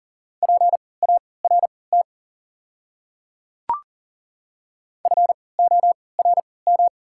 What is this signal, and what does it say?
Morse code, heard as a keyed tone, about 40 wpm, spaced as for 23 wpm, 690 Hz. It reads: PART FORM
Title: 3x Character Spacing - Top 200 Words - 40wpm